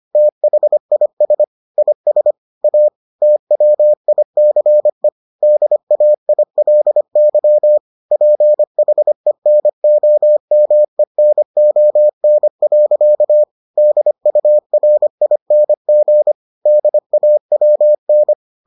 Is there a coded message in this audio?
Yes: Morse code from 0.2 to 18.3 s, heard as a keyed tone, about 25 wpm, 600 Hz. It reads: THIS IS A TWICE DAILY PHENOMENON. DURING DAWN